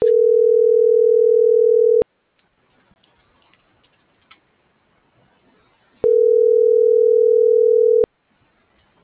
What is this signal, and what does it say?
Ringback tone (North America), 440 + 480 Hz.